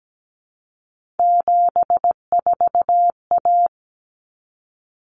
Morse code, heard as a keyed tone, about 17 wpm, 700 Hz. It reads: 74A
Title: Common Cut Numbers - 3-Digit 17wpm